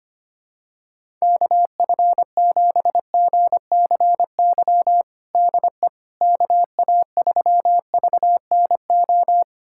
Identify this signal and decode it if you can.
Morse code, heard as a keyed tone, about 25 wpm, 700 Hz. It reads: KF7GCY DE KA3VNO
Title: USA Callsign DE Callsign - 6 Characters - 25wpm